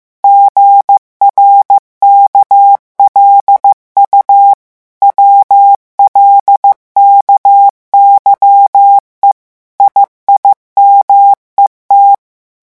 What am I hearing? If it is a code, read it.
Morse code, heard as a keyed tone, about 15 wpm, 800 Hz. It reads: GRKLU WLKYE IIMET
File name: morse mp3 / revision7-vitesse 14 mots